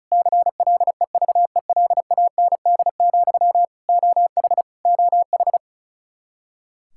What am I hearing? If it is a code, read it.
Morse code, heard as a keyed tone, about 36 wpm, 700 Hz. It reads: CLEVELAND, OH OH
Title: Practice QSO - Exchange 1 - Beginning - 35wpm